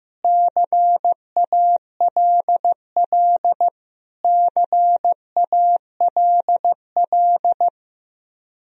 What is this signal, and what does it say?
Morse code, heard as a keyed tone, about 15 wpm, 700 Hz. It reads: CALL CALL
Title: Top 100 Words Repeated 15wpm